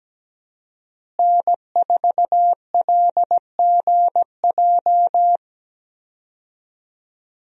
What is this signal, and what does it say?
Morse code, heard as a keyed tone, about 17 wpm, 700 Hz. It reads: N4LGJ
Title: USA Call Signs - 5 Characters - Repeated 3x - 17wpm